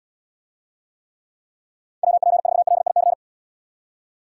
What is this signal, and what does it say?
Morse code, heard as a keyed tone, about 72 wpm, 700 Hz. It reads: WP4LIC